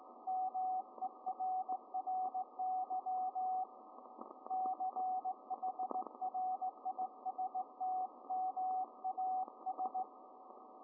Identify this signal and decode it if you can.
Morse code, heard as a keyed tone, about 17 wpm, 750 Hz, with radio noise. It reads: MERRY CHRISTMAS